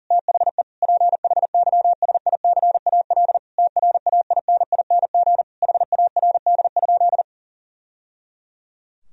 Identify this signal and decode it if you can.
Morse code, heard as a keyed tone, about 40 wpm, 700 Hz. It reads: THE PHYSICAL TRAINING HARD?
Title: Sentences from Top 1000 Words - Review Part 4 - 40wpm